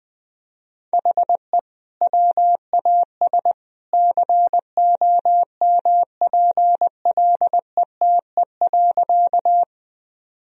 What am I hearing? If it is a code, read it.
Morse code, heard as a keyed tone, about 21 wpm, 700 Hz. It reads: HE WAS COMPLETE.
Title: Sentences from Top 1000 Words - Review Part 2 - 20wpm